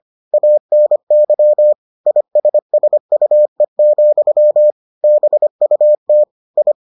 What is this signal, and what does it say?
Morse code, heard as a keyed tone, about 25 wpm, 600 Hz. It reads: ANY ISSUE, BUT I